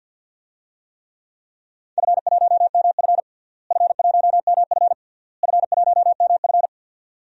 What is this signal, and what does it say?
Morse code, heard as a keyed tone, about 46 wpm, 700 Hz. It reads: F1MF F1MF F1MF
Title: DXCC-100 Call Signs - 4 Characters - Repeated 3x - 50wpm